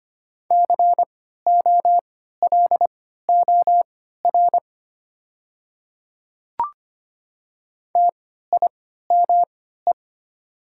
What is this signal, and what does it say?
Morse code, heard as a keyed tone, about 25 wpm, spaced as for 14 wpm, 700 Hz. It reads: COLOR TIME
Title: 3x Character Spacing - Top 500 Words - 25wpm